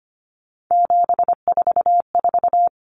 Morse code, heard as a keyed tone, about 25 wpm, 700 Hz. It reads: 744
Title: Common Cut Numbers - 3-Digit 25wpm